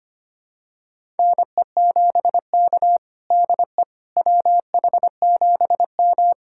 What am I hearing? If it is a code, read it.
Morse code, heard as a keyed tone, about 25 wpm, 700 Hz. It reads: NE7K DE WH7M
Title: USA Callsign DE Callsign - 4 Characters - 25wpm